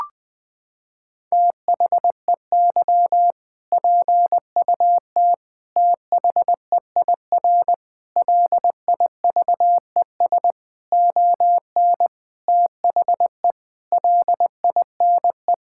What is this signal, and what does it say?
Morse code, heard as a keyed tone, about 20 wpm, 700 Hz. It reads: THEY PUT THEIR LIVES ON THE LINE
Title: Sentences with 8-Word Phrases - 20wpm